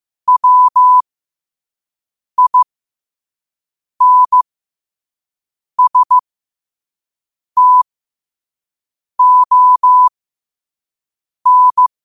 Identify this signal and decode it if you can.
Morse code, heard as a keyed tone, about 15 wpm, spaced as for 5 wpm, 1000 Hz. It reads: WINSTON